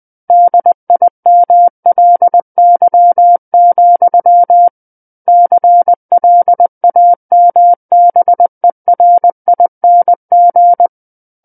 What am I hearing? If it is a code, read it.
Morse code, heard as a keyed tone, about 20 wpm, 700 Hz. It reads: DIMLY, CLAMBERING
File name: 20 WPM